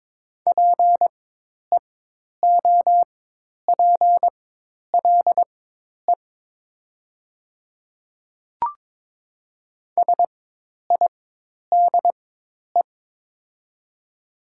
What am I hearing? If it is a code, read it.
Morse code, heard as a keyed tone, about 22 wpm, spaced as for 10 wpm, 700 Hz. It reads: PEOPLE SIDE
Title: 4x Character Spacing - Top 100 Words - 22wpm